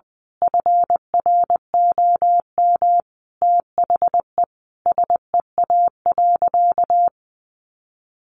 Morse code, heard as a keyed tone, about 20 wpm, 700 Hz. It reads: FROM THE SEA.
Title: Sentences from Top 1400 Words 20wpm